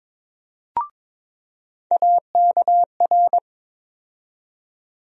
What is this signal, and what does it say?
Morse code, heard as a keyed tone, about 22 wpm, 700 Hz. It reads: AKR
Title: USA Call Sign Postfixes - 3 Letters 22wpm